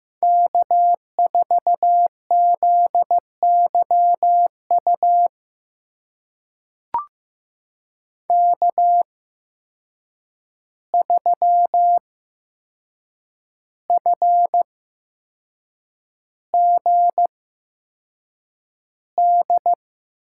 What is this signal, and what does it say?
Morse code, heard as a keyed tone, about 15 wpm, spaced as for 6 wpm, 700 Hz. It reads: K4ZYU K3FGD
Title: USA Call Signs - 5 Characters - 8x Character Spacing - 15wpm